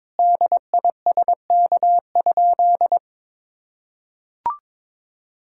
Morse code, heard as a keyed tone, about 22 wpm, 700 Hz. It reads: DISK?
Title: Sentences from Top 1000 Words - Review Part 6 - 22wpm